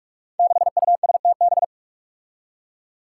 Morse code, heard as a keyed tone, about 46 wpm, 700 Hz. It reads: 6USTB
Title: Five Challenging Letter-Number Combinations 45wpm